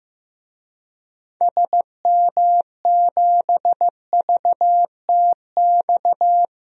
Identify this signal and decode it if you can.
Morse code, heard as a keyed tone, about 15 wpm, 700 Hz. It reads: SM7VTX
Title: DXCC-100 Call Signs - 6 Characters - 15wpm